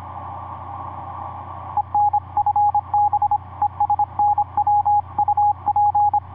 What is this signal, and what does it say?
Morse code, heard as a keyed tone, about 26 wpm, 830 Hz. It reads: EN FB ES NW UP
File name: vu2pai ik5pwq etc 17m cw 16jan2010 1003